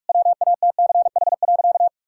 Morse code, heard as a keyed tone, about 41 wpm, 700 Hz. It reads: WATCH.
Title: Sentences from Top 1600 Words 45wpm